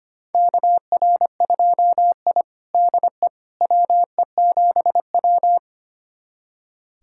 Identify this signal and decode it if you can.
Morse code, heard as a keyed tone, about 25 wpm, 700 Hz. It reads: KR2I DE WE7W
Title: USA Callsign DE Callsign - 4 Characters - 25wpm